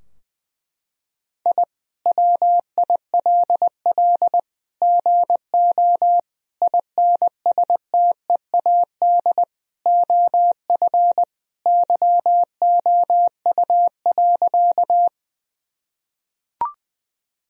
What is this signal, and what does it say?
Morse code, heard as a keyed tone, about 20 wpm, 700 Hz. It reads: I WILL GO INSTEAD OF YOU.
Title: Sentences from Top 1000 Words - Review Part 4 - 20wpm